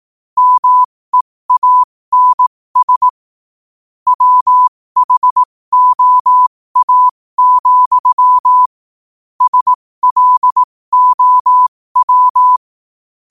Morse code, heard as a keyed tone, about 18 wpm, spaced as for 15 wpm, 1000 Hz. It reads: MEANS WHOA, SLOW
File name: qotdfiles / 15wpm